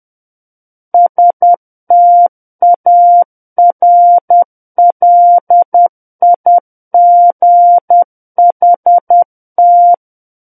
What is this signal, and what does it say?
Morse code, heard as a keyed tone, about 10 wpm, 700 Hz. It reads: STARLIGHT
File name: Book One / 01-Chapter 1 - 10 WPM